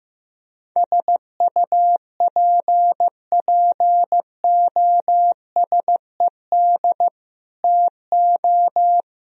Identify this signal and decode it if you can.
Morse code, heard as a keyed tone, about 16 wpm, 700 Hz. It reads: SUPPOSED TO
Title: Speed-Racing - Sets of 3 Words 15wpm